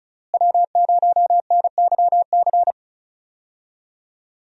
Morse code, heard as a keyed tone, about 34 wpm, 700 Hz. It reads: W0NYC
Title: USA Call Signs - 5 Characters - Repeated 2x - 35wpm